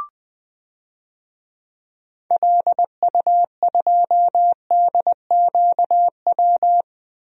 Morse code, heard as a keyed tone, about 20 wpm, 700 Hz. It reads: LU2DQW
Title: DXCC-100 Call Signs - 6 Characters - Repeated 3x - 20wpm